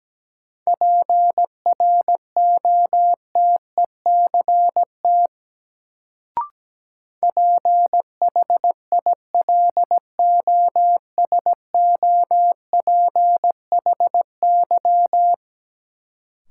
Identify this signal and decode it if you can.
Morse code, heard as a keyed tone, about 17 wpm, 700 Hz. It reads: PROTECT PHILOSOPHY
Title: Sentences from Top 900 Words - New Word Drill - 17wpm